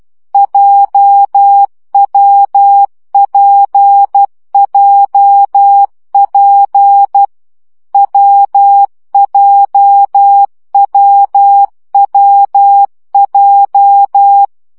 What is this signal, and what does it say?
Morse code, heard as a keyed tone, about 12 wpm, 800 Hz. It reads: JWPJP WJWWJ
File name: CW gyakorlas w p j 10 baud